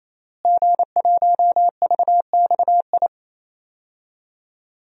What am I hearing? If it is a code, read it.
Morse code, heard as a keyed tone, about 28 wpm, 700 Hz. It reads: G1VXI
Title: DXCC-100 Call Signs - 5 Characters - Repeated 3x - 28wpm